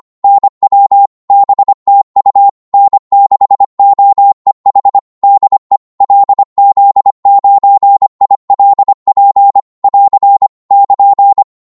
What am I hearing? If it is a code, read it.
Morse code, heard as a keyed tone, about 25 wpm, 800 Hz. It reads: NW BTU N6OEH DE LZ9ILP <AR> <KN>
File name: Morse / qso-25wpm